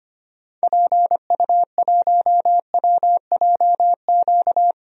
Morse code, heard as a keyed tone, about 25 wpm, 700 Hz. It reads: PU1WJQ
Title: DXCC-100 Call Signs - 6 Characters - 25wpm